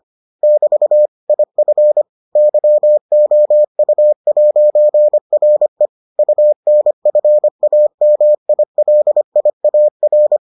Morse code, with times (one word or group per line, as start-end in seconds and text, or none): 0.4-1.1 <BT>
1.3-2.0 IF
2.4-5.9 YOU'RE
6.2-10.4 UNFAMILIAR